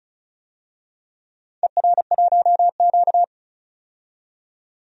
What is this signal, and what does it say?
Morse code, heard as a keyed tone, about 36 wpm, 700 Hz. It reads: ER1Q
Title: DXCC-100 Call Signs - 4 Characters - 35wpm